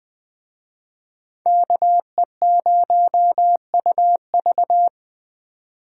Morse code, heard as a keyed tone, about 20 wpm, 700 Hz. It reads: KE0UV